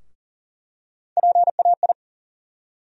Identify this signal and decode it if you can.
Morse code, heard as a keyed tone, about 40 wpm, 700 Hz. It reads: PAI
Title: USA Call Sign Postfixes - 3 Letters 40wpm